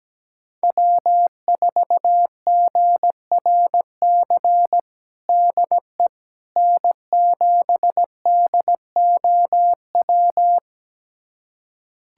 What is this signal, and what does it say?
Morse code, heard as a keyed tone, about 17 wpm, 700 Hz. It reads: W4GRC DE N7DOW
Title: USA Callsign DE Callsign - 5 Characters - 17wpm